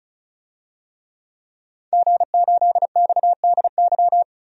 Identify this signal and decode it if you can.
Morse code, heard as a keyed tone, about 34 wpm, 700 Hz. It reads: G8XDY
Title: DXCC-100 Call Signs - 5 Characters - Repeated 3x - 35wpm